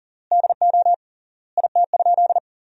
Morse code, heard as a keyed tone, about 40 wpm, 700 Hz. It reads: DO IT?
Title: 3x Word-Spacing - Sentences from Top 200 Words - 40wpm